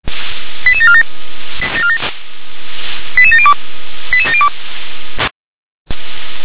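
Five-tone selective call (ZVEI), heard as quick runs of five tones, tones of about 70 ms. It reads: 88747 88702 88702
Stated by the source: ZVEI five-tone call received via SDR.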